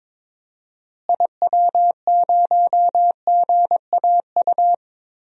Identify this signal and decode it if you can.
Morse code, heard as a keyed tone, about 22 wpm, 700 Hz. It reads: IW0GAU
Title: DXCC-100 Call Signs - 6 Characters - Repeated 3x - 22wpm